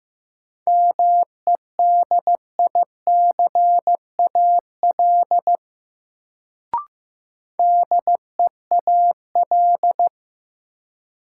Morse code, heard as a keyed tone, about 15 wpm, 700 Hz. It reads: MEDICAL DEAL
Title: Sentences from Top 800 Words - New Word Drill - 15wpm